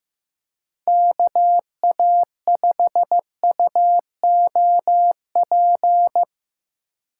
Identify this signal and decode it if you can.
Morse code, heard as a keyed tone, about 15 wpm, 700 Hz. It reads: KA5UOP
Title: USA Call Signs - 6 Characters - 3x Character Spacing - 15wpm